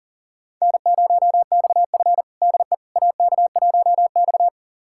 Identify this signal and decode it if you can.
Morse code, heard as a keyed tone, about 37 wpm, 700 Hz. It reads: N0XF DE AK1X
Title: USA Callsign DE Callsign - 4 Characters - 40wpm